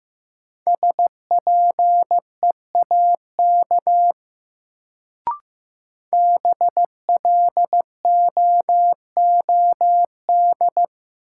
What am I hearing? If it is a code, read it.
Morse code, heard as a keyed tone, about 15 wpm, 700 Hz. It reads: SPEAK BLOOD
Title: Sentences from Top 1000 Words - Review Part 2 - New Word Drill - 15wpm